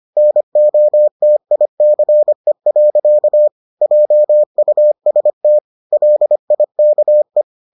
Morse code, heard as a keyed tone, about 25 wpm, 600 Hz. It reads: NOTICE. JUST LIKE